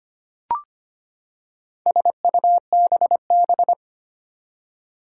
Morse code, heard as a keyed tone, about 25 wpm, 700 Hz. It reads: SUBB
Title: Four Challenging Letter-Number Combinations 25wpm